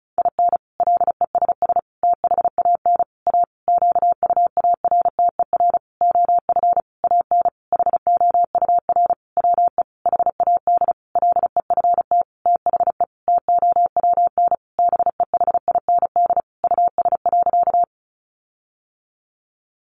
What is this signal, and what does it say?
Morse code, heard as a keyed tone, about 34 wpm, 700 Hz. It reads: IN LESS THAN A QUARTER OF AN HOUR WE HAD LEFT THE TOWN BEHIND US.
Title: Sentences from Top 1300 Words 35wpm